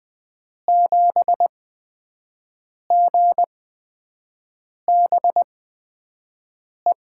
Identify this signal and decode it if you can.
Morse code, heard as a keyed tone, about 20 wpm, spaced as for 5 wpm, 700 Hz. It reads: 7GBE
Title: USA Call Signs - 6 Characters - 8x Character Spacing - 20wpm